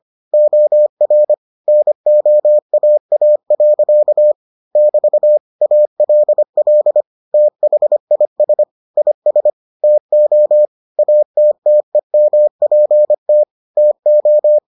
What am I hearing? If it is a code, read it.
Morse code, heard as a keyed tone, about 25 wpm, 600 Hz. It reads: OR NOAA. <BT> ALL THIS IS TO ATTEMPT TO